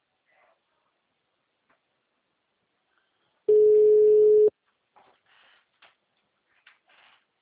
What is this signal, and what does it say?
Ringback tone (Europe, 425 Hz).
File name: out- -2014- - - .16126